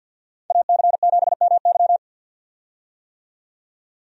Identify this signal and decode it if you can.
Morse code, heard as a keyed tone, about 52 wpm, 700 Hz. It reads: AC7MY
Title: USA Call Signs - 5 Characters - 50wpm